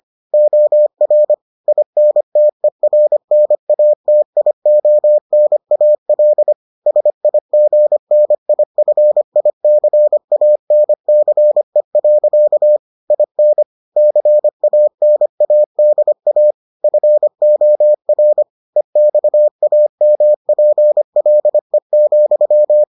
Morse code, heard as a keyed tone, about 25 wpm, 600 Hz. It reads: OR INTERNATIONAL SIGNIFICANCE. IN CANADA FOR EXAMPLE,